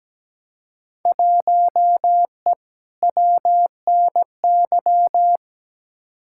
Morse code, heard as a keyed tone, about 17 wpm, 700 Hz. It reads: 1E WNY
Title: Field Day Exchanges - 17wpm